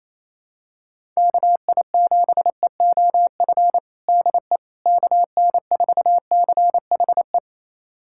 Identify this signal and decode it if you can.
Morse code, heard as a keyed tone, about 28 wpm, 700 Hz. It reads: KI7EOF DE KN4CHE
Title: USA Callsign DE Callsign - 6 Characters - 28wpm